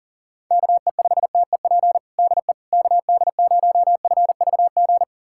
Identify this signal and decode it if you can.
Morse code, heard as a keyed tone, about 40 wpm, 700 Hz. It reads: KE5TEP DE KD0FVG
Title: USA Callsign DE Callsign - 6 Characters - 40wpm